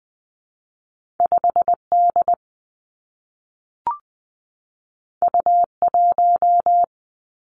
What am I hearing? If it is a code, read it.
Morse code, heard as a keyed tone, about 20 wpm, 700 Hz. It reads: 5 D U 1
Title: Two Challenging Letter-Number Combinations 20wpm